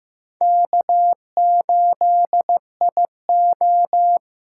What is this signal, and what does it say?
Morse code, heard as a keyed tone, about 15 wpm, 700 Hz. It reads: K8IO
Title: USA Call Signs - 4 Characters - 6x Character Spacing - 15wpm